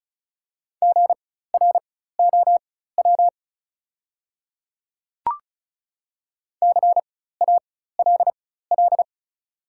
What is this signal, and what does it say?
Morse code, heard as a keyed tone, about 34 wpm, spaced as for 16 wpm, 700 Hz. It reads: GROW CALL
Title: 4x Character Spacing - Top 300 Words - 35wpm